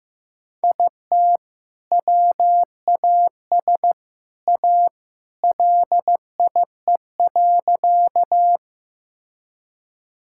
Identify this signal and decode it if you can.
Morse code, heard as a keyed tone, about 15 wpm, 700 Hz. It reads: IT WAS A LIE.
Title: Sentences from Top 1600 Words 15wpm